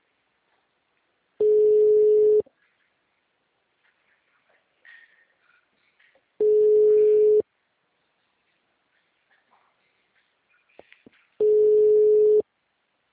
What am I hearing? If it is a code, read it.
Ringback tone (Europe, 425 Hz).
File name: out- -2003- - - .75343